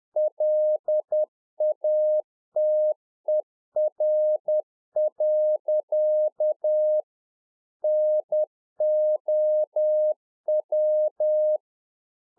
Morse code, heard as a keyed tone, about 10 wpm, 610 Hz. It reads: LATER. NOW
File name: UNFOGETTABLE QSO 10WPM